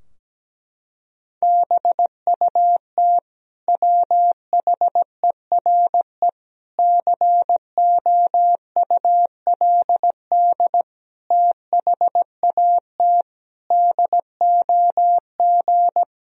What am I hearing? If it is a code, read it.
Morse code, heard as a keyed tone, about 17 wpm, 700 Hz. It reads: BUT WHERE COULD THAT DOG
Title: Sentences from Top 1000 Words - Review Part 2 - 17wpm